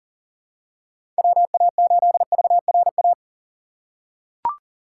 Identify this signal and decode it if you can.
Morse code, heard as a keyed tone, about 40 wpm, 700 Hz. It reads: WA8VRA
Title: USA Call Signs - 6 Characters - 2x Character Spacing - 40wpm